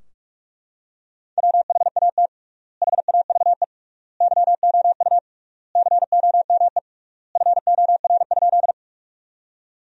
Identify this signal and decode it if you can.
Morse code, heard as a keyed tone, about 43 wpm, 700 Hz. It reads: WHAT HAVE YOU COME FOR?
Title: 3x Word-Spacing - Sentences from Top 100 Words - 45wpm